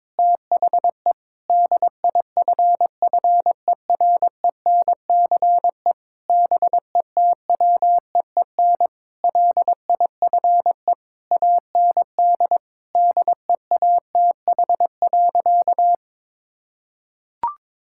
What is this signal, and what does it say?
Morse code, heard as a keyed tone, about 22 wpm, 700 Hz. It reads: THE DIFFERENCE BETWEEN LIFE AND DEATH.
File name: practice / 22wpm / Sentences with 8-Word Phrases - 22wpm